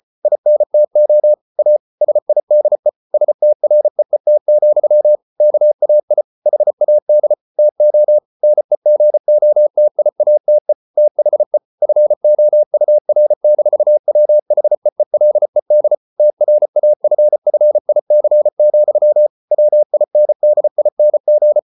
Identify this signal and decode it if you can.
Morse code, heard as a keyed tone, about 34 wpm, 600 Hz. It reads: INTO A SIDE STREET, KAI HAD TO NEGOTIATE THE FOUR-WHEELED TRAFFIC, WINDING